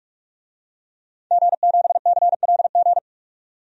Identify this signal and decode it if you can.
Morse code, heard as a keyed tone, about 41 wpm, 700 Hz. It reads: G7CLG
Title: DXCC-100 Call Signs - 5 Characters - 45wpm